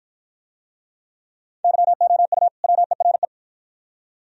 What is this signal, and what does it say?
Morse code, heard as a keyed tone, about 47 wpm, 700 Hz. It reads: YOU WERE